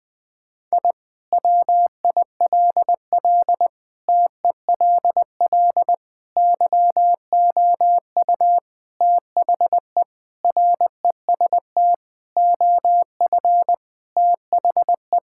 Morse code, heard as a keyed tone, about 20 wpm, 700 Hz. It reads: I WILL TELL YOU THE REST OF THE
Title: Sentences with 4-Word Phrases - 20wpm